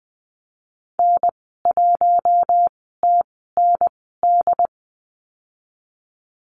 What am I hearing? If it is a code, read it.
Morse code, heard as a keyed tone, about 20 wpm, spaced as for 15 wpm, 700 Hz. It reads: N1TND
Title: 088 Lesson 15g - Number 5 - Call Signs 20wpm